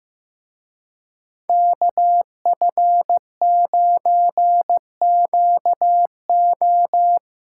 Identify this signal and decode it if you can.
Morse code, heard as a keyed tone, about 16 wpm, 700 Hz. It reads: KF9QO
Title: Speed Racing - USA 2x2 Call Signs 15wpm